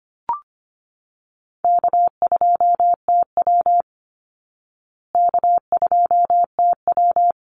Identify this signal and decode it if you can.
Morse code, heard as a keyed tone, about 25 wpm, 700 Hz. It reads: K2TW K2TW
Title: Easy Call Signs Repeated 3x - 25wpm